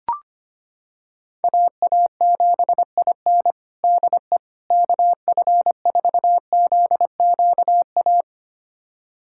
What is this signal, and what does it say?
Morse code, heard as a keyed tone, about 25 wpm, 700 Hz. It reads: AA7IN DE KF4ZQA
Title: USA Callsign DE Callsign 25wpm